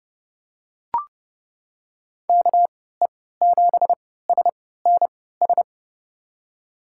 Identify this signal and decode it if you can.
Morse code, heard as a keyed tone, about 30 wpm, spaced as for 17 wpm, 700 Hz. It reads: KE7SNS
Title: USA Call Signs - 6 Characters - 3x Character Spacing - 30wpm